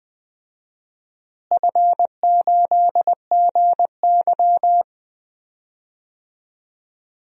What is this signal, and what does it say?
Morse code, heard as a keyed tone, about 20 wpm, 700 Hz. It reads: F8GY
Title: DXCC-100 Call Signs - 4 Characters - Repeated 2x - 20wpm